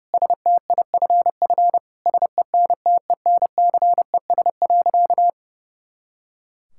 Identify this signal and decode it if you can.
Morse code, heard as a keyed tone, about 30 wpm, 700 Hz. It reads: STIFF SENTENCES.
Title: Pangram Challenge v2 - 30wpm